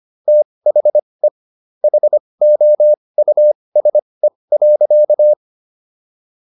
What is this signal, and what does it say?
Morse code, heard as a keyed tone, about 25 wpm, spaced as for 20 wpm, 600 Hz. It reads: THE HOUSE.